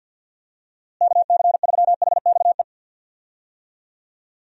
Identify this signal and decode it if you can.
Morse code, heard as a keyed tone, about 46 wpm, 700 Hz. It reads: KC3HXE
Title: USA Call Signs - 6 Characters - Repeated 3x - 50wpm